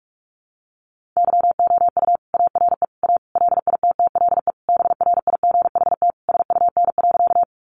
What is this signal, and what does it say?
Morse code, heard as a keyed tone, about 44 wpm, 700 Hz. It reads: YOU ARE A LITTLE BRIGHT SUN.
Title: Sentences from Top 1000 Words 45wpm